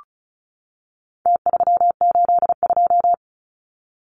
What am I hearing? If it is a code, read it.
Morse code, heard as a keyed tone, about 34 wpm, 700 Hz. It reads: T382
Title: Numbers - Contest Style - 4-Digit 35wpm